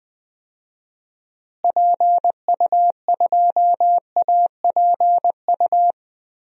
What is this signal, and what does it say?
Morse code, heard as a keyed tone, about 20 wpm, 700 Hz. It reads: PU2APU
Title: DXCC-100 Call Signs - 6 Characters - Repeated 2x - 20wpm